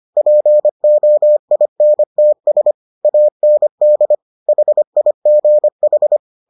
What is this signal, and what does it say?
Morse code, heard as a keyed tone, about 25 wpm, 600 Hz. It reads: POINTS AND HIGH